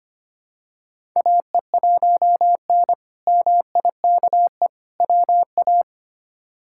Morse code, heard as a keyed tone, about 25 wpm, 700 Hz. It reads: AE1N MIKE WA
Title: K1USN SST QSO Practice - 25wpm